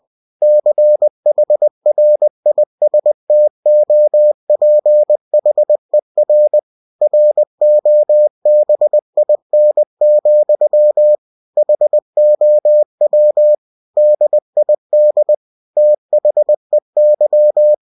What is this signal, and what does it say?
Morse code, heard as a keyed tone, about 20 wpm, 600 Hz. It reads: CHRISTOPHER ROBIN, HOW DID THEY